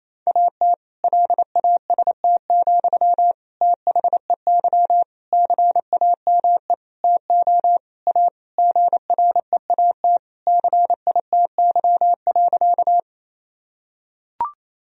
Morse code, heard as a keyed tone, about 28 wpm, 700 Hz. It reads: AT LAST, THEY CAME TO A GREAT CITY.
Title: Sentences from Top 1000 Words - Review Part 1 - 28wpm